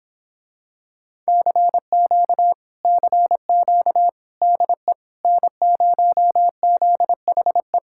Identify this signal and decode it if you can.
Morse code, heard as a keyed tone, about 26 wpm, 700 Hz. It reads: CQ CQ DE N0ZHE